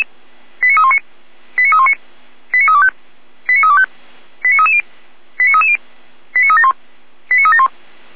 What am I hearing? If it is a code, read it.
Five-tone selective call (ZVEI), heard as quick runs of five tones, tones of about 70 ms. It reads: 89319 89319 89325 89325 89330 89330 89371 89371